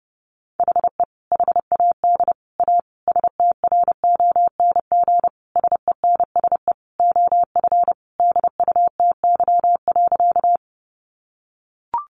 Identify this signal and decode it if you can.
Morse code, heard as a keyed tone, about 30 wpm, 700 Hz. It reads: HE HAD A STRONG SENSE OF DUTY.
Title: Sentences from Top 1500 Words 30wpm